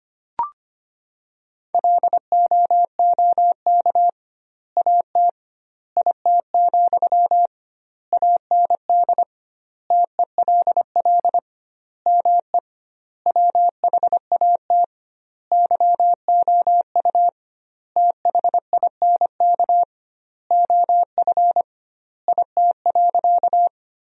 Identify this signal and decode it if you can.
Morse code, heard as a keyed tone, about 25 wpm, 700 Hz. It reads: LOOK AT IT, AND TELL ME WHAT YOU THINK OF IT.
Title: 2x Word-Spacing - Sentences from Top 200 Words - 25wpm